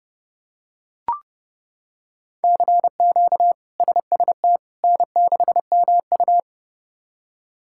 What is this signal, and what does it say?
Morse code, heard as a keyed tone, about 32 wpm, 700 Hz. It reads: CQ SST N6MU
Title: K1USN SST QSO Practice - 30wpm